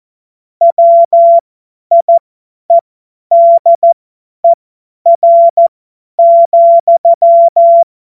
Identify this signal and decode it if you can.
Morse code, heard as a keyed tone, about 14 wpm, spaced as for 10 wpm, 680 Hz. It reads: WIEDER,
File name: Morsekurs / Witze / Witz 10. Januar 2023 50 F70